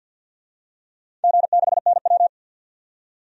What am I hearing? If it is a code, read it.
Morse code, heard as a keyed tone, about 48 wpm, 700 Hz. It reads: G6NW
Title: DXCC-100 Call Signs - 4 Characters - 50wpm